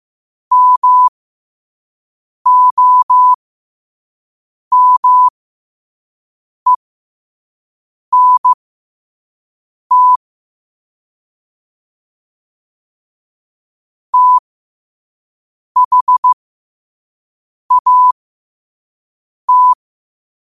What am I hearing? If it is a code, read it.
Morse code, heard as a keyed tone, about 15 wpm, spaced as for 5 wpm, 1000 Hz. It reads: MOMENT THAT
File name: qotdfiles / 5wpm